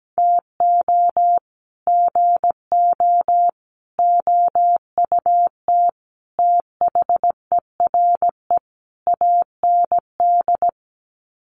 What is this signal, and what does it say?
Morse code, heard as a keyed tone, about 17 wpm, 700 Hz. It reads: TO GO OUT THERE AND
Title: Sets of 5 Words Encore2 17wpm